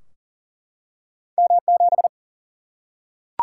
Morse code, heard as a keyed tone, about 40 wpm, 700 Hz. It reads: M7
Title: Single-Letter Single-Number 40wpm